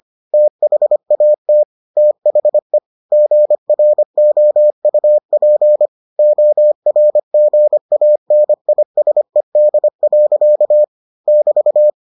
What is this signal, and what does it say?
Morse code, heard as a keyed tone, about 25 wpm, 600 Hz. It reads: THAT THE GROUP ORGANISED. <BT>